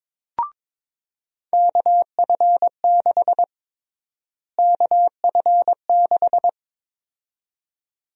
Morse code, heard as a keyed tone, about 22 wpm, 700 Hz. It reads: KF6 KF6
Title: USA Call Sign Prefixes - Repeated 2x - 22wpm